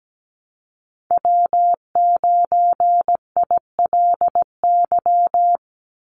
Morse code, heard as a keyed tone, about 17 wpm, 700 Hz. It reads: W9ILY
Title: Call Signs - 17wpm